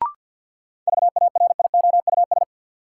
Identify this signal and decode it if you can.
Morse code, heard as a keyed tone, about 53 wpm, 700 Hz. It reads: VARIOUS
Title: Sentences from Top 1000 Words - Review Part 4 - New Word Drill - 50wpm